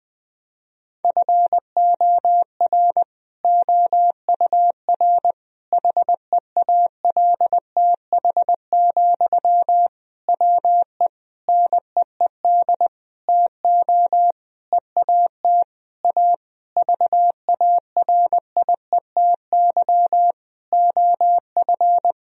Morse code, heard as a keyed tone, about 20 wpm, 700 Hz. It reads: FOR OUR HEALTH, WE NEED TO EAT A VARIETY OF
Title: Sentences from Top 1000 Words - Review Part 4 - 20wpm